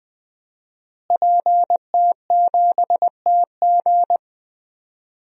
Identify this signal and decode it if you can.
Morse code, heard as a keyed tone, about 21 wpm, 700 Hz. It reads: PT7TG
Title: DXCC-100 Call Signs - 5 Characters - Repeated 2x - 20wpm